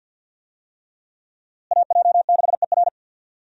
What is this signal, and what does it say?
Morse code, heard as a keyed tone, about 50 wpm, 700 Hz. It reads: AJ6ER
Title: USA Call Signs - 5 Characters - Repeated 3x - 50wpm